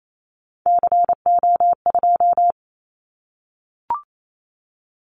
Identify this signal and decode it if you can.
Morse code, heard as a keyed tone, about 28 wpm, 700 Hz. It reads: CO2
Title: Callsign Prefixes for Top 100 DXCC - 28wpm